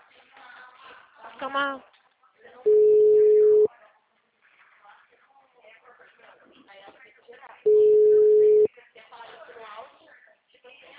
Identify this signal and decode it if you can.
Ringback tone (Europe, 425 Hz), with speech.